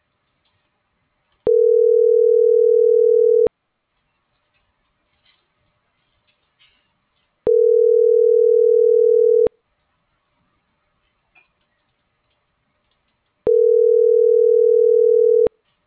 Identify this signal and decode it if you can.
Ringback tone (North America), 440 + 480 Hz.